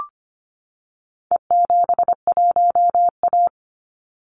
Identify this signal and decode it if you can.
Morse code, heard as a keyed tone, about 25 wpm, 700 Hz. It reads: E71A
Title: Call Signs - 25wpm